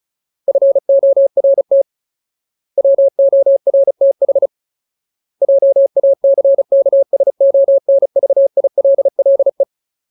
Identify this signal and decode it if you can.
Morse code, heard as a keyed tone, about 34 wpm, 550 Hz. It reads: FORT WORTH JACKSONVILLE
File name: Ultimate Word List / 35WPM Ultimate